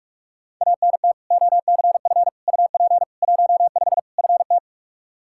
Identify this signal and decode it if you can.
Morse code, heard as a keyed tone, about 46 wpm, 700 Hz. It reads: ANT OCF UP 15 FT
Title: Practice QSO - Exchange 2 - Info - 45wpm